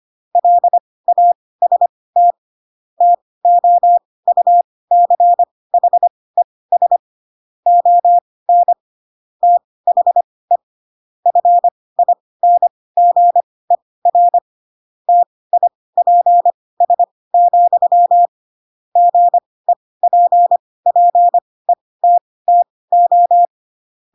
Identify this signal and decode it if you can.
Morse code, heard as a keyed tone, about 25 wpm, spaced as for 18 wpm, 700 Hz. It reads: LAST TOUCHES ON THE FINGER TIPS, GEPPETTO
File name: Pinocchio ch3 12 25 ik1hge